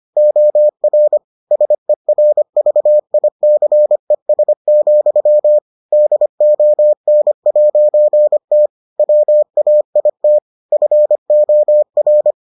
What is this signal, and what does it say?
Morse code, heard as a keyed tone, about 25 wpm, 600 Hz. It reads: OR SERVICES, DON'T WAIT FOR